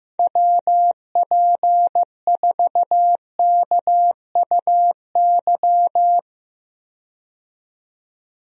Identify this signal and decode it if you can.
Morse code, heard as a keyed tone, about 15 wpm, 700 Hz. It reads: WP4KUY
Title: USA Call Signs - 6 Characters - 15wpm